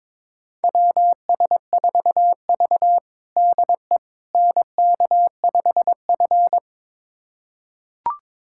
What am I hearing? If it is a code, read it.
Morse code, heard as a keyed tone, about 22 wpm, 700 Hz. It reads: WS4V DE NK5F
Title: USA Callsign DE Callsign - 4 Characters - 22wpm